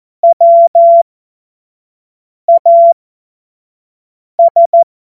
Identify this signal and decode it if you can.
Morse code, heard as a keyed tone, about 14 wpm, spaced as for 5 wpm, 680 Hz. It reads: WAS
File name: Morsekurs / Witze / Witz 24.Oktober 2023 25 F70